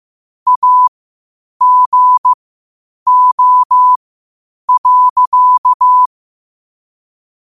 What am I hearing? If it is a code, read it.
Morse code, heard as a keyed tone, about 15 wpm, spaced as for 8 wpm, 1000 Hz. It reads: AGO.
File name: qotdfiles / 8wpm